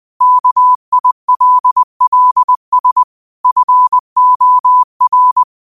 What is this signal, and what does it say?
Morse code, heard as a keyed tone, about 20 wpm, 1000 Hz. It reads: KILLS FOR
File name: qotdfiles / 20wpm